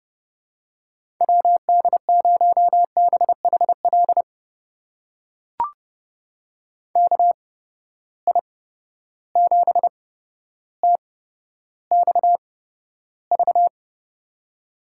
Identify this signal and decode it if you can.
Morse code, heard as a keyed tone, about 28 wpm, spaced as for 13 wpm, 700 Hz. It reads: WD0BHL KI7TXV
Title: USA Call Signs - 6 Characters - 8x Character Spacing - 30wpm